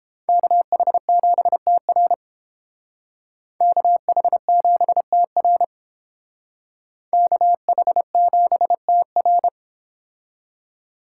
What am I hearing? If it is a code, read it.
Morse code, heard as a keyed tone, about 28 wpm, 700 Hz. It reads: KH7TR KH7TR KH7TR